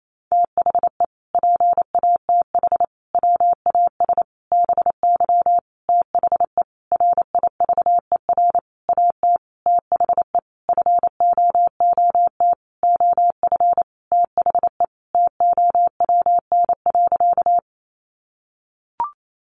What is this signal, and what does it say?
Morse code, heard as a keyed tone, about 28 wpm, 700 Hz. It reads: THE PATH WAS BY THE RIVER AT THE FOOT OF THE TOWN.
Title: Sentences from Top 1400 Words 28wpm